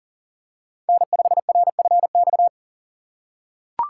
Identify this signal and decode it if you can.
Morse code, heard as a keyed tone, about 40 wpm, 700 Hz. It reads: N5RFX
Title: USA Call Signs - 5 Characters - 3x Character Spacing - 40wpm